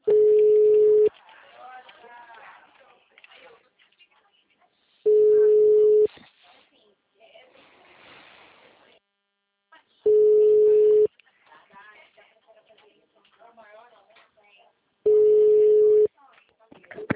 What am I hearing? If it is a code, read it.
Ringback tone (Europe, 425 Hz), with speech.
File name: out- -2002- - - .54872